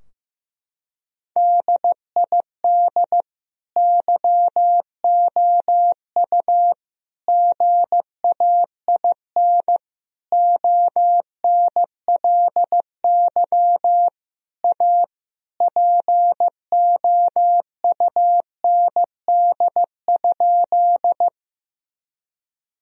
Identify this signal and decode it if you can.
Morse code, heard as a keyed tone, about 15 wpm, 700 Hz. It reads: DID YOU GAIN ONLY A POUND?
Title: Sentences from Top 1000 Words - Review Part 2 - 15wpm